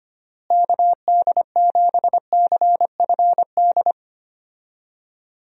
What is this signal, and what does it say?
Morse code, heard as a keyed tone, about 25 wpm, 700 Hz. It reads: KD7CFD
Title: USA Call Signs - 6 Characters - Repeated 3x - 25wpm